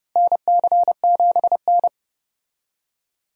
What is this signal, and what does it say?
Morse code, heard as a keyed tone, about 30 wpm, 700 Hz. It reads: NC7N
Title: USA Call Signs - 4 Characters - Repeated 3x - 30wpm